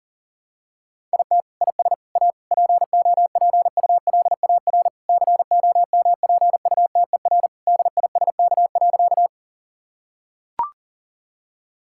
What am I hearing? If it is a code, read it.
Morse code, heard as a keyed tone, about 40 wpm, 700 Hz. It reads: IT IS A POPULAR COMPUTER DISK.
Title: Sentences from Top 1000 Words - Review Part 6 - 40wpm